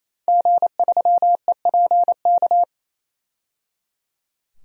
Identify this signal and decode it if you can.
Morse code, heard as a keyed tone, about 27 wpm, 700 Hz. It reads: G3EPK
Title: DXCC-100 Call Signs - 5 Characters - 28wpm